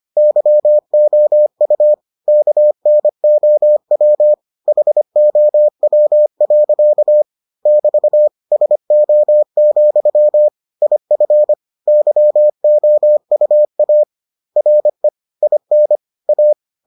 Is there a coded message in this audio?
Yes: Morse code from 0.2 to 16.5 s, heard as a keyed tone, about 25 wpm, 600 Hz. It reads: YOU KNOW HOW. <BT> SO, IF YOUA RE IN A